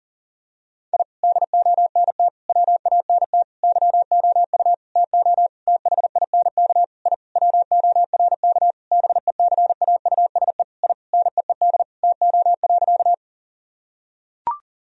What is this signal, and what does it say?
Morse code, heard as a keyed tone, about 40 wpm, 700 Hz. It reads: I DONT WANT YOU TO THINK I WORK BECAUSE I NEED TO.
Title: Sentences from Top 1000 Words - Review Part 3 - 40wpm